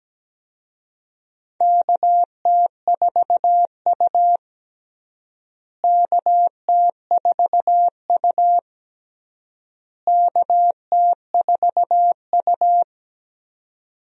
Morse code, heard as a keyed tone, about 17 wpm, 700 Hz. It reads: KT4U KT4U KT4U
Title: USA Call Signs - 4 Characters - Repeated 3x - 17wpm